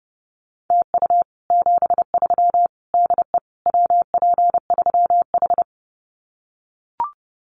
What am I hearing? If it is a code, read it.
Morse code, heard as a keyed tone, about 30 wpm, 700 Hz. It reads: TU 73 DE WP3H
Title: Common QSO Phrases Encore 30wpm